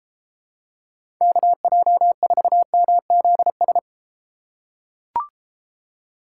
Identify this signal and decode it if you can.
Morse code, heard as a keyed tone, about 29 wpm, 700 Hz. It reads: KJ4MZS